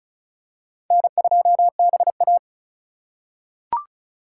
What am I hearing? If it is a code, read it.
Morse code, heard as a keyed tone, about 34 wpm, 700 Hz. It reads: N2BA
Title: USA Call Signs - 4 Characters - 4x Character Spacing - 35wpm